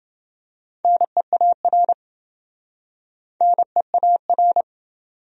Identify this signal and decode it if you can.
Morse code, heard as a keyed tone, about 28 wpm, 700 Hz. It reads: NEAR NEAR